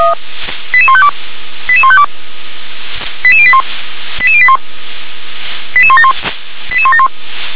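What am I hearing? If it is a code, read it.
Five-tone selective call (ZVEI), heard as quick runs of five tones, tones of about 70 ms. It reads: 88152 88152 88081 88081 88171 88171
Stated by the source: ZVEI five-tone selective call received by SDR, code 88162.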